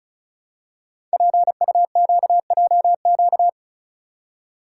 Morse code, heard as a keyed tone, about 36 wpm, 700 Hz. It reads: PUQJQ
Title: Code Groups - 5 Letters - 35wpm